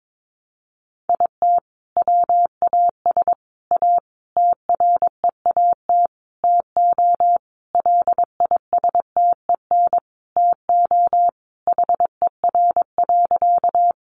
Morse code, heard as a keyed tone, about 22 wpm, 700 Hz. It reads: IT WAS A TREAT TO LISTEN TO HER.
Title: Sentences from Top 1300 Words 22wpm